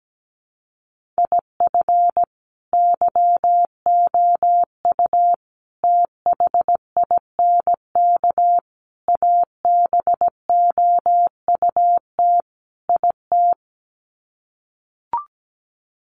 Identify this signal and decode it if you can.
Morse code, heard as a keyed tone, about 17 wpm, 700 Hz. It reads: IF YOU THINK ABOUT IT
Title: Sets of 5 Words Encore2 17wpm